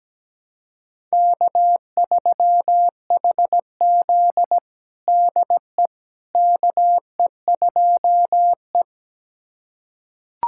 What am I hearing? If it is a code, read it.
Morse code, heard as a keyed tone, about 17 wpm, 700 Hz. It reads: K3HZ DE KE2E
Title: USA Callsign DE Callsign - 4 Characters - 17wpm